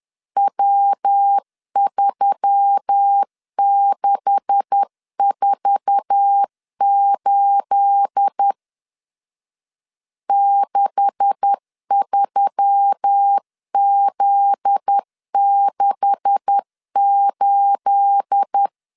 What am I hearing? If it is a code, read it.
Morse code, heard as a keyed tone, about 10 wpm, 800 Hz. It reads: W3648 63Z68
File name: LEZIONI / 9 286 10wpm 2735